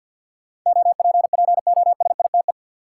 Morse code, heard as a keyed tone, about 52 wpm, 700 Hz. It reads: OPPOSITE
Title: Sentences from Top 1300 Words - New Word Drill Alt - 50wpm